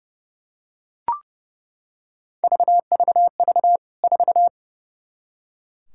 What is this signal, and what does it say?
Morse code, heard as a keyed tone, about 30 wpm, 700 Hz. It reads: VVV 4
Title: K1USN SST QSO Practice 30wpm